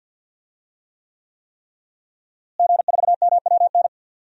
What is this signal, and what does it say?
Morse code, heard as a keyed tone, about 52 wpm, 700 Hz. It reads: G4MWN
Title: DXCC-100 Call Signs - 5 Characters - Repeated 3x - 50wpm